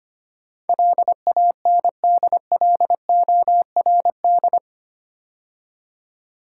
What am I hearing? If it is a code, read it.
Morse code, heard as a keyed tone, about 25 wpm, 700 Hz. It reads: LANDLORD
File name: 8 Letter Words - First 3 Letters Given - 25wpm